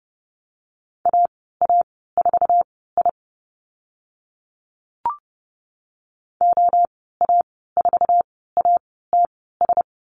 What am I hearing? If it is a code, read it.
Morse code, heard as a keyed tone, about 30 wpm, spaced as for 17 wpm, 700 Hz. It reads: AA4I OA4ATS
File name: Lesson 14g - Number 2 - Call Signs 30wpm